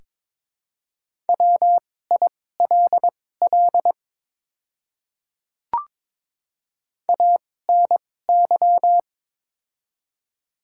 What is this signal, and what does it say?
Morse code, heard as a keyed tone, about 21 wpm, spaced as for 16 wpm, 700 Hz. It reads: WILL ANY
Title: 2x Character Spacing - Top 200 Words - 22wpm